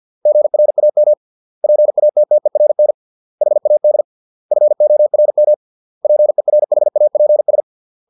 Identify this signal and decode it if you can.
Morse code, heard as a keyed tone, about 50 wpm, 600 Hz. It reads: GRAM PATTERN HAD FORM PERHAPS